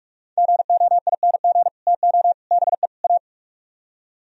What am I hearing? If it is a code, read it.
Morse code, heard as a keyed tone, about 57 wpm, 700 Hz. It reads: GOING TO BE A